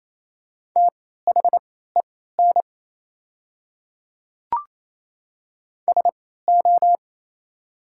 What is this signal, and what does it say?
Morse code, heard as a keyed tone, about 27 wpm, spaced as for 16 wpm, 700 Hz. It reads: THEN SO
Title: 3x Character Spacing - Top 100 Words - 28wpm